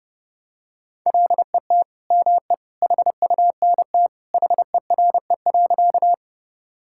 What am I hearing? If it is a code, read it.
Morse code, heard as a keyed tone, about 32 wpm, 700 Hz. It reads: LET ME HUNT HERE.
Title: Sentences from Top 1600 Words 30wpm